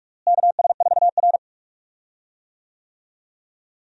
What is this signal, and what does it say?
Morse code, heard as a keyed tone, about 41 wpm, 700 Hz. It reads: KS4R
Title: USA Call Signs - 4 Characters - Repeated 3x - 45wpm